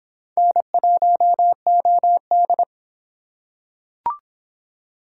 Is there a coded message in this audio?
Yes: Morse code from 0.4 to 2.6 s, heard as a keyed tone, about 25 wpm, 700 Hz. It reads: N1OD